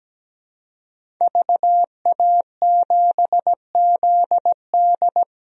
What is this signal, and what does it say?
Morse code, heard as a keyed tone, about 17 wpm, 700 Hz. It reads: VA7ZD
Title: DXCC-100 Call Signs - 5 Characters - Repeated 3x - 17wpm